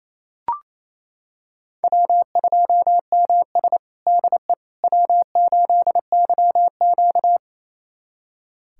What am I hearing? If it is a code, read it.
Morse code, heard as a keyed tone, about 28 wpm, 700 Hz. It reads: W2MS DE W8YQ
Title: USA Callsign DE Callsign - 4 Characters - 28wpm